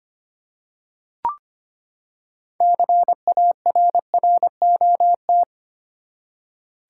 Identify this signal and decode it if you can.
Morse code, heard as a keyed tone, about 25 wpm, 700 Hz. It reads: CARROT
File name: practice / 25wpm / Easy Words with Hints - 25wpm